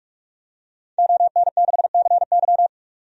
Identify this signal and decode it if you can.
Morse code, heard as a keyed tone, about 46 wpm, 700 Hz. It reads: ON6CY
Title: DXCC-100 Call Signs - 5 Characters - 45wpm